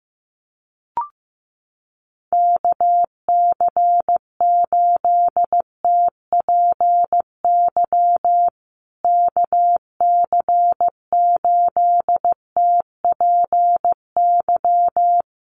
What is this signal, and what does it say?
Morse code, heard as a keyed tone, about 14 wpm, 700 Hz. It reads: KC8TPY KC8TPY
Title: USA Call Signs Repeated 3x 15wpm